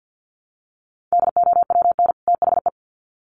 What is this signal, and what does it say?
Morse code, heard as a keyed tone, about 48 wpm, 700 Hz. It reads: DOWN THE
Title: Sets of 2 Words 50wpm